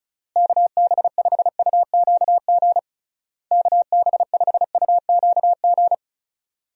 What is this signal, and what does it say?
Morse code, heard as a keyed tone, about 36 wpm, 700 Hz. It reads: KB5UQG KB5UQG
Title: USA Call Signs - 6 Characters - Repeated 3x - 35wpm